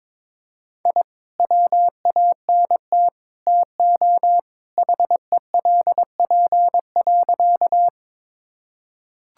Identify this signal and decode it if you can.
Morse code, heard as a keyed tone, about 22 wpm, 700 Hz. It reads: I WANT TO HELP.
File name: practice / 22wpm / Sentences with 3-Word Phrases - 22wpm